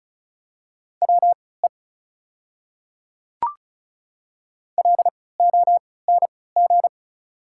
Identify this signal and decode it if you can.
Morse code, heard as a keyed tone, about 34 wpm, spaced as for 20 wpm, 690 Hz. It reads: WE LONG
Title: 3x Character Spacing - Top 100 Words - 35wpm